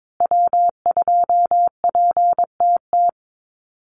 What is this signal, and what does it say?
Morse code, heard as a keyed tone, about 22 wpm, 700 Hz. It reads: W2PTT
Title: Call Signs Repeated 2x - 22wpm